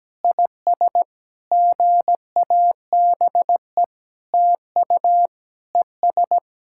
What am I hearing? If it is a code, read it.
Morse code, heard as a keyed tone, about 17 wpm, 700 Hz. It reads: IS GABE TU ES
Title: SOTA QSO Practice 17wpm